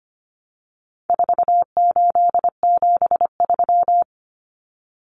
Morse code, heard as a keyed tone, about 25 wpm, 700 Hz. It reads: 4873
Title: Common Cut Numbers - 4-Digit 25wpm